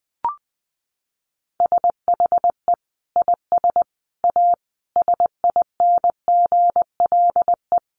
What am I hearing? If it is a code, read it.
Morse code, heard as a keyed tone, about 20 wpm, 700 Hz. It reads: SHE IS A SINGLE
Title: Sentences from Top 700 Words 20wpm